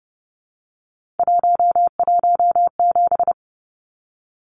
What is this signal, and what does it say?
Morse code, heard as a keyed tone, about 29 wpm, 700 Hz. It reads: 117
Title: Numbers - 3-Digit 30wpm